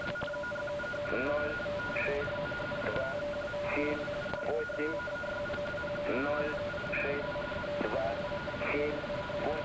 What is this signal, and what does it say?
Two-tone teleprinter signal, about 850 Hz shift, with speech.